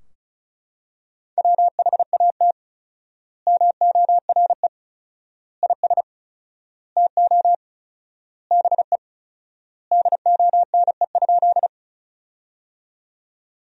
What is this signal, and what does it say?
Morse code, heard as a keyed tone, about 34 wpm, 700 Hz. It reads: WHAT MORE IS TO BE DONE?
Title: 4x Word-Spacing - Sentences from Top 500 Words - 35wpm